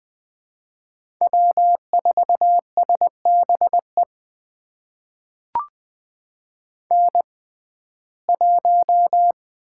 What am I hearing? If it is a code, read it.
Morse code, heard as a keyed tone, about 20 wpm, spaced as for 14 wpm, 700 Hz. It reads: W4SBE N1
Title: USA Call Signs - 5 Characters - 6x Character Spacing - 20wpm